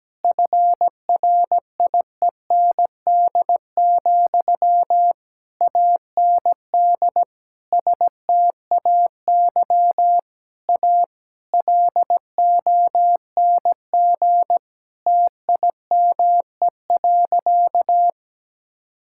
Morse code, heard as a keyed tone, about 17 wpm, 700 Hz. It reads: FRIEND, AND STAY A LONG TIME.
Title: Sentences from Top 1000 Words - Review Part 6 - 17wpm